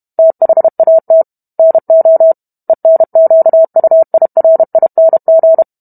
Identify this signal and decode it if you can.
Morse code, heard as a keyed tone, about 32 wpm, 650 Hz. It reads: THAT NO ENQUIRING